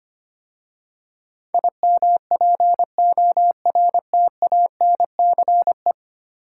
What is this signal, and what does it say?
Morse code, heard as a keyed tone, about 25 wpm, 700 Hz. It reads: IMPORTANCE
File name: 10 Letter Words - First 3 Letters Given - 25wpm